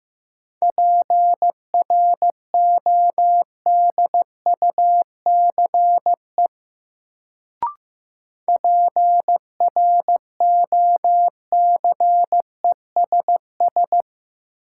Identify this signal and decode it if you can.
Morse code, heard as a keyed tone, about 15 wpm, 700 Hz. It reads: PRODUCE PROCESS
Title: Sentences from Top 1000 Words - Review Part 3 - New Word Drill - 15wpm